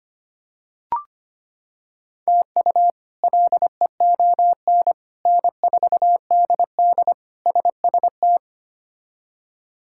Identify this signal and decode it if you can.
Morse code, heard as a keyed tone, about 25 wpm, 700 Hz. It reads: TU LEON N4DD SST
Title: K1USN SST QSO Practice - 25wpm